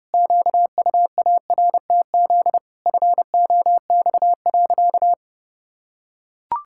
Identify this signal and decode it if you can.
Morse code, heard as a keyed tone, about 32 wpm, 700 Hz. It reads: QUARTZ FOX.
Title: Pangram Challenge v2 - 30wpm